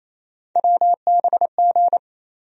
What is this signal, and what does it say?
Morse code, heard as a keyed tone, about 27 wpm, 700 Hz. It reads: WBG
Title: USA Call Sign Postfixes - 3 Letters 28wpm